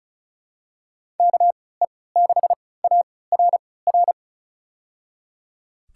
Morse code, heard as a keyed tone, about 36 wpm, spaced as for 20 wpm, 700 Hz. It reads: KE6ARR
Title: USA Call Signs - 6 Characters - 3x Character Spacing - 35wpm